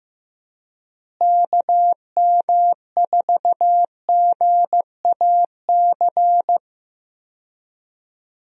Morse code, heard as a keyed tone, about 15 wpm, 700 Hz. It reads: KM4GAC
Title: USA Call Signs - 6 Characters - Repeated 3x - 15wpm